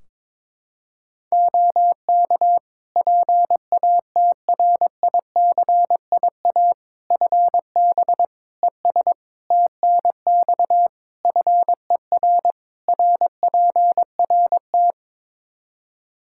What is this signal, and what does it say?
Morse code, heard as a keyed tone, about 22 wpm, 700 Hz. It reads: OK PATRICIA FB ES TNX FER RPRT
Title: Practice QSO - Exchange 2 - Info - 22wpm